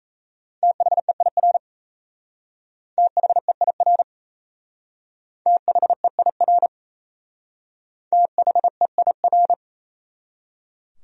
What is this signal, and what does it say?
Morse code, heard as a keyed tone, about 33 wpm, 700 Hz. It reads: THEIR THEIR THEIR THEIR
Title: Speed-Racing Top 500 Words 28wpm